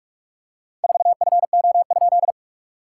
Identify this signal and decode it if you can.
Morse code, heard as a keyed tone, about 45 wpm, 690 Hz. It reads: 4FO?
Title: SOTA QSO Practice 45wpm